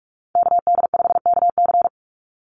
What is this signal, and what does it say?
Morse code, heard as a keyed tone, about 46 wpm, 690 Hz. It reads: KD5KC
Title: NA SOTA Chaser Callsigns 45wpm